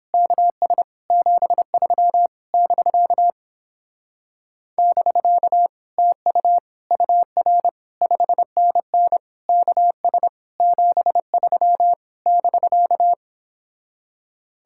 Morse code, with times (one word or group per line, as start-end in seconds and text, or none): 0.1-0.8 KS
1.1-2.3 73
2.5-3.3 <BK>
4.8-5.7 <BK>
6.0-6.6 TU
6.9-7.7 UR
8.0-9.2 5NN
9.5-10.3 KS
10.6-11.9 73
12.3-13.1 <BK>